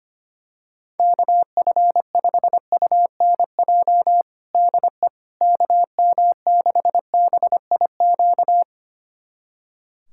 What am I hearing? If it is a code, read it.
Morse code, heard as a keyed tone, about 25 wpm, 700 Hz. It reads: KF5UNJ DE KM6BIQ
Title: USA Callsign DE Callsign 25wpm